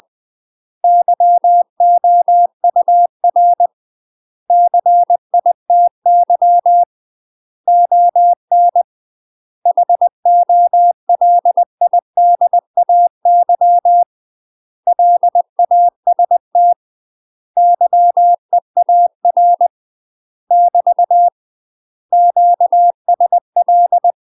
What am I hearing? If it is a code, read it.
Morse code, heard as a keyed tone, about 20 wpm, 700 Hz. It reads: YOUR CITY ON HOLIDAY LAST YEAR <BT> QSL
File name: UK MASTER QSOs TRAINING / 20WPM / UK MASTER QSOs 0065